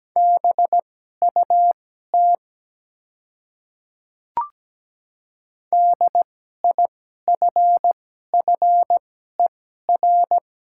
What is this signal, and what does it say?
Morse code, heard as a keyed tone, about 17 wpm, spaced as for 12 wpm, 700 Hz. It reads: BUT DIFFER
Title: 2x Character Spacing - Top 300 Words - 17wpm